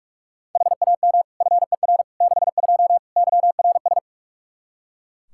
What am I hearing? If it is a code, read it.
Morse code, heard as a keyed tone, about 42 wpm, 700 Hz. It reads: HAM FER 62 YRS
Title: Practice QSO - Exchange 3 - More Info - 45wpm